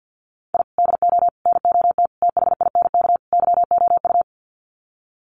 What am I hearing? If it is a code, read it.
Morse code, heard as a keyed tone, about 48 wpm, 700 Hz. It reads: I DO NOT THINK YOU
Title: Sets of 5 Words 50wpm